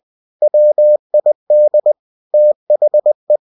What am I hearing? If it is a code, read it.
Morse code, heard as a keyed tone, about 20 wpm, 600 Hz. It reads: WID THE